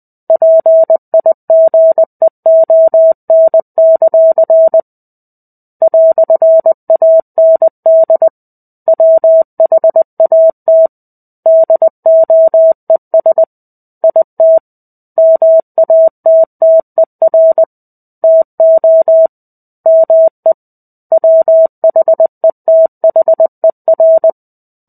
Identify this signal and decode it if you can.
Morse code, heard as a keyed tone, about 20 wpm, 650 Hz. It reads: PIGEON; "AND WHAT DOES IT MATTER TO ME WHETHER
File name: 05-Chapter-05 20 WPM